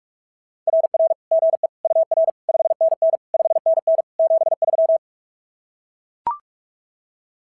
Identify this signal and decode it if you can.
Morse code, heard as a keyed tone, about 44 wpm, 660 Hz. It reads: RR GE UR 5NN 5NN 73
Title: SOTA QSO Practice 45wpm